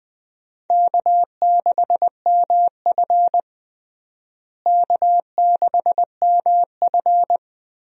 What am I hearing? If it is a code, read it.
Morse code, heard as a keyed tone, about 21 wpm, 700 Hz. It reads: K6MF K6MF
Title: USA Call Signs - 4 Characters - Repeated 3x - 20wpm